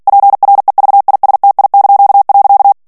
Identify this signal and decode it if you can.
Morse code, heard as a keyed tone, about 48 wpm, 800 Hz. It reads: PREVISTI).
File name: ITALIEN texte clair 2 vitesse 45 mots minute